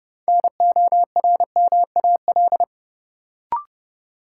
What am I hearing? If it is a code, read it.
Morse code, heard as a keyed tone, about 31 wpm, 700 Hz. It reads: NORMAL
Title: Sentences from Top 1200 Words - New Word Drill - 30wpm